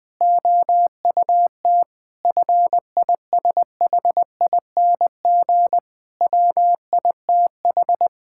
Morse code, heard as a keyed tone, about 20 wpm, 700 Hz. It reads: OUT FISHING WITH
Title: Sentences from Top 1000 Words - Review Part 5 - 20wpm